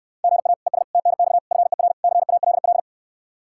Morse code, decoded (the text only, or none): GA ES TNX FER CALL